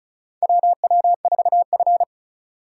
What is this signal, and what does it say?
Morse code, heard as a keyed tone, about 34 wpm, 700 Hz. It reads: WW4F
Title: USA Call Signs - 3x Character Spacing - 35wpm